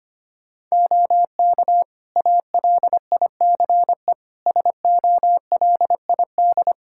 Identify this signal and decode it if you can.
Morse code, heard as a keyed tone, about 25 wpm, 700 Hz. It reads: OK ALICE SOLID
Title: Practice QSO - Exchange 3 - More Info - 25wpm